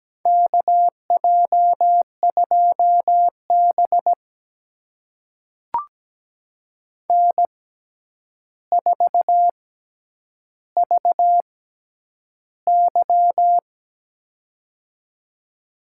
Morse code, heard as a keyed tone, about 17 wpm, spaced as for 9 wpm, 700 Hz. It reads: KJ2B N4VY
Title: USA Call Signs - 4 Characters - 6x Character Spacing - 17wpm